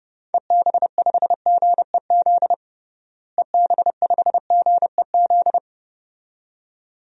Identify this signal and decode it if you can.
Morse code, heard as a keyed tone, about 30 wpm, 700 Hz. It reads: EB5GEZ EB5GEZ
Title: DXCC-100 Call Signs - 6 Characters - Repeated 2x - 30wpm